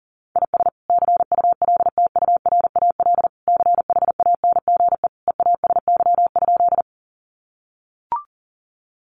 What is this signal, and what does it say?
Morse code, heard as a keyed tone, about 40 wpm, 700 Hz. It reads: IS CULTURAL CHANGE EASY?
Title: Sentences from Top 1500 Words 40wpm